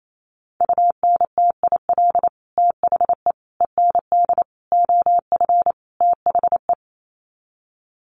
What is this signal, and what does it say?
Morse code, heard as a keyed tone, about 28 wpm, 700 Hz. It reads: UNTIL THE END OF THE
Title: Sets of 5 Words Encore 28wpm